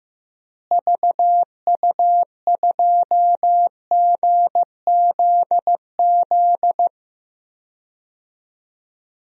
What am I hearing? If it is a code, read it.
Morse code, heard as a keyed tone, about 16 wpm, 700 Hz. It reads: VU2GZZ
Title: DXCC-100 Call Signs - 6 Characters - Repeated 3x - 15wpm